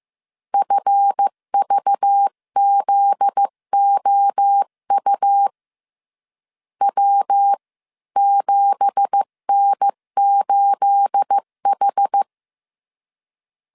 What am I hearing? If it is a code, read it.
Morse code, heard as a keyed tone, about 15 wpm, 800 Hz. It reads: FVZOU W7N8H
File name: CW / DET / 1 9 1500C S B 14wpm 3302